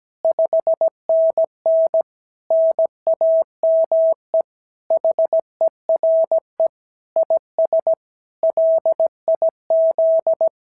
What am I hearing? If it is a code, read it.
Morse code, heard as a keyed tone, about 16 wpm, 650 Hz. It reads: 5NN NAME HERE IS LIZ
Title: SOTA QSO Practice 17wpm